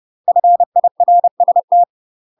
Morse code, heard as a keyed tone, about 30 wpm, 700 Hz. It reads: FIRST
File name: UK MASTER QSOs TRAINING / 30WPM / UK MASTER QSOs 0050